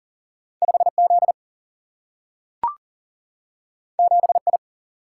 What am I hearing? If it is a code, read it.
Morse code, heard as a keyed tone, about 39 wpm, 700 Hz. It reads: 5Z 7I
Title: Single-Letter Single-Number 40wpm